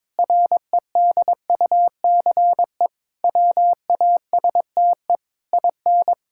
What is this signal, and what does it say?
Morse code, heard as a keyed tone, about 22 wpm, 700 Hz. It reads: REDUCE WASTE IN
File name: Difficult Sentences with Hints - 22wpm